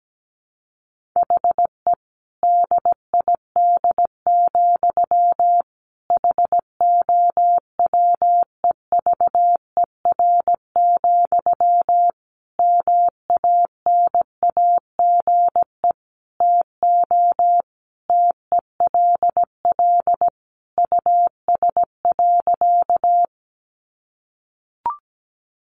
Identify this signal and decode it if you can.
Morse code, heard as a keyed tone, about 17 wpm, 700 Hz. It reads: HE DID, HOWEVER, MANAGE TO TELL US.
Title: Sentences from Top 1400 Words 17wpm